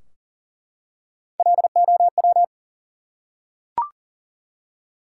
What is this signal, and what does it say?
Morse code, heard as a keyed tone, about 40 wpm, 700 Hz. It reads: LOW
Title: Three Letter Words - 40wpm